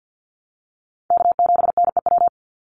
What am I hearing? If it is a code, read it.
Morse code, heard as a keyed tone, about 50 wpm, 700 Hz. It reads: K7NEW
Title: NA SOTA Chaser Callsigns 50wpm